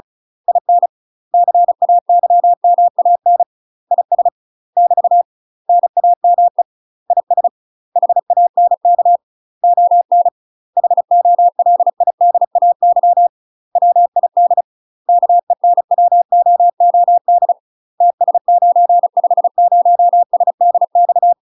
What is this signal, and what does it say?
Morse code, heard as a keyed tone, about 36 wpm, 700 Hz. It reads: IN CAYMAN IS <BT> NAME IS HANK ON HOLIDAY WID KENWOOD TS950SDX